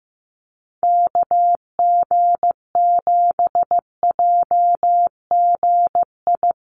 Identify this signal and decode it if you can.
Morse code, heard as a keyed tone, about 15 wpm, 700 Hz. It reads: KG7JGI
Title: USA Call Signs Repeated 3x 15wpm